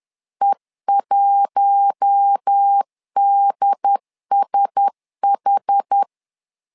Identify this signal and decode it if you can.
Morse code, heard as a keyed tone, about 10 wpm, 800 Hz. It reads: E1DSH
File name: LEZIONI / 5 LBVC / 5 L 10wpm 2259